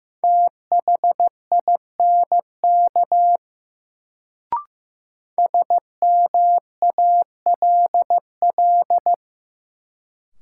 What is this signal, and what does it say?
Morse code, heard as a keyed tone, about 15 wpm, 700 Hz. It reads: THINK SMALL
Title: Sentences from Top 200 Words - New Word Drill - 15wpm